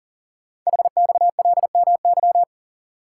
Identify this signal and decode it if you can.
Morse code, heard as a keyed tone, about 41 wpm, 690 Hz. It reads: HXLMY
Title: Code Groups - 5 Letters - 40wpm